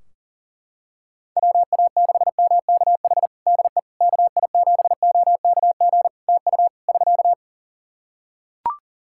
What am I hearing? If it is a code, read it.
Morse code, heard as a keyed tone, about 40 wpm, 700 Hz. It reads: WA6MKH DE KI7OKG TU <SK>
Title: Practice QSO - Exchange 4 - Ending - 40wpm